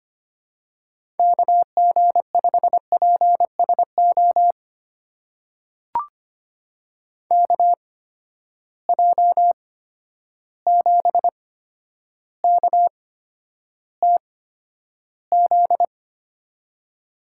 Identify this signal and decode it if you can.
Morse code, heard as a keyed tone, about 25 wpm, spaced as for 11 wpm, 700 Hz. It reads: KG5PSO KJ7KTZ
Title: USA Call Signs - 8x Character Spacing - 25wpm